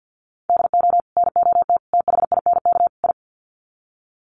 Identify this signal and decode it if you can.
Morse code, heard as a keyed tone, about 50 wpm, 700 Hz. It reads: DO NOT THINK I
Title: Sets of 4 Words 50wpm